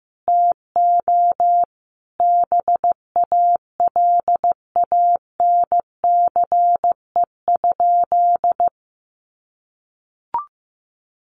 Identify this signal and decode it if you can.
Morse code, heard as a keyed tone, about 15 wpm, 700 Hz. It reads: TO BALANCE?
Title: Sentences from Top 900 Words 15wpm